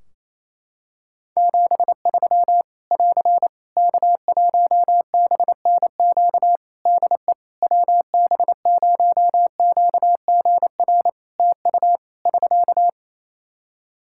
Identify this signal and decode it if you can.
Morse code, heard as a keyed tone, about 28 wpm, 700 Hz. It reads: 73 <AR> K1BNQ DE WB0QGR TU <SK>
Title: Practice QSO - Exchange 4 - Ending - 28wpm